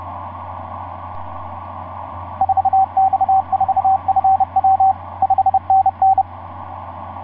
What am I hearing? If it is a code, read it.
Morse code, heard as a keyed tone, about 30 wpm, 770 Hz, with radio noise. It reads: 4X4FW 5NN
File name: s79gm f6eyb etc 12m cw 24mar2010 1207